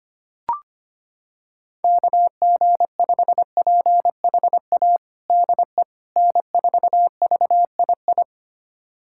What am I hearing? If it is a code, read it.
Morse code, heard as a keyed tone, about 25 wpm, 700 Hz. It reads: KG5PHA DE N4VII
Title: USA Callsign DE Callsign 25wpm